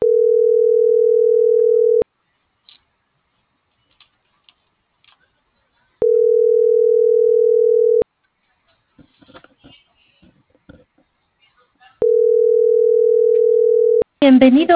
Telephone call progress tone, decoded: ringback tone

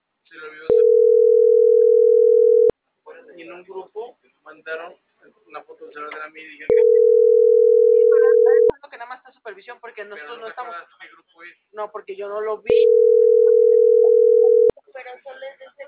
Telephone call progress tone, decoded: ringback tone

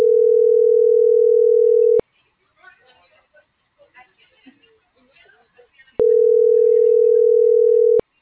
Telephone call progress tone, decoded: ringback tone

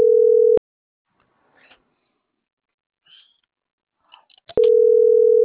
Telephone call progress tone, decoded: ringback tone

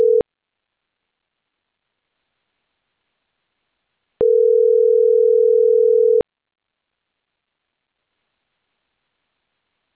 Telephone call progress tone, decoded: ringback tone

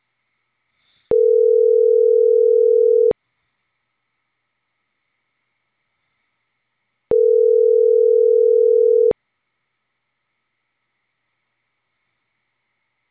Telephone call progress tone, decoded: ringback tone

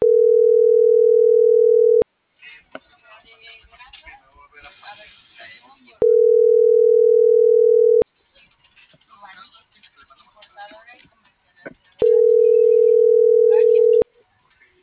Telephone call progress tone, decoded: ringback tone

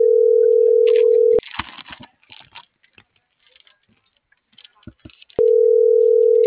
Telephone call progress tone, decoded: ringback tone

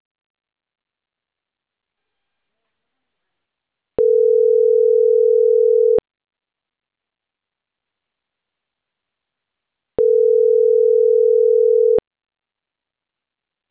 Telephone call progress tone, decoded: ringback tone

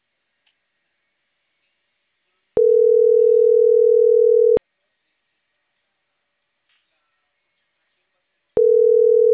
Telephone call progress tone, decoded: ringback tone

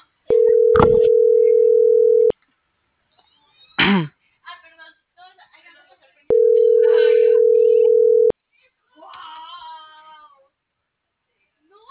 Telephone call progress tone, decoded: ringback tone

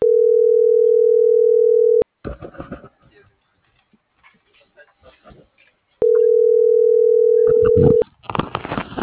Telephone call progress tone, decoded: ringback tone

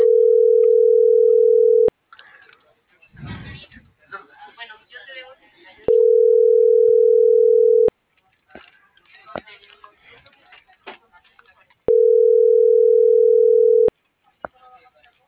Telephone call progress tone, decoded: ringback tone

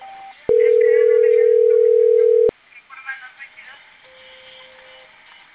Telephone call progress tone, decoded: ringback tone

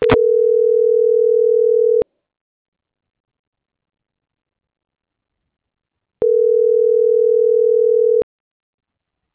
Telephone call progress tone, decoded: ringback tone

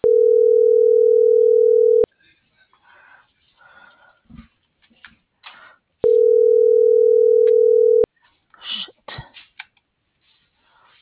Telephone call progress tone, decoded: ringback tone

